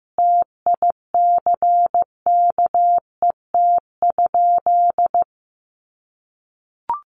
Morse code, heard as a keyed tone, about 15 wpm, 700 Hz. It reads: TICKET?
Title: Sentences from Top 1400 Words 15wpm